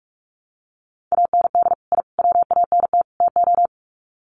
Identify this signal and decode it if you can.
Morse code, heard as a keyed tone, about 44 wpm, 700 Hz. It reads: AND I WANT TO
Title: Sets of 4 Words 45wpm